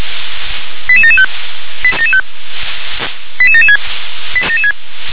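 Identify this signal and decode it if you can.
Five-tone selective call (ZVEI), heard as quick runs of five tones, tones of about 70 ms. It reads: 88775 88776
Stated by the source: ZVEI five-tone selective call received by SDR.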